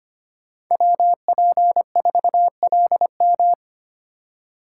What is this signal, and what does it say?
Morse code, heard as a keyed tone, about 25 wpm, 700 Hz. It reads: WP4LM
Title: USA Call Signs - 5 Characters - 2x Character Spacing - 25wpm